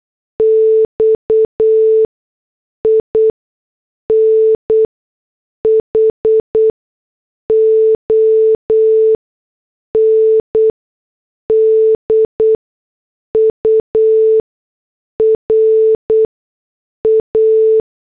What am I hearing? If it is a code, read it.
Morse code, heard as a keyed tone, about 8 wpm, spaced as for 6 wpm, 440 Hz. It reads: X IN HONDURA